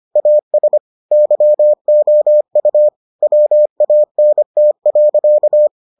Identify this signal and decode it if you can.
Morse code, heard as a keyed tone, about 25 wpm, 600 Hz. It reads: AS YOU WANT.